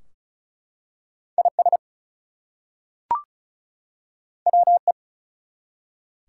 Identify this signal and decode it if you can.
Morse code, heard as a keyed tone, about 33 wpm, 700 Hz. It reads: IS WE
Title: Two Letter Words - 35wpm